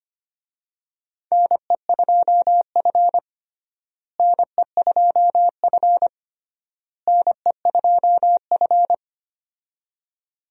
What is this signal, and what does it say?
Morse code, heard as a keyed tone, about 25 wpm, 700 Hz. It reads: NE2F NE2F NE2F
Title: USA Call Signs - 4 Characters - Repeated 3x - 25wpm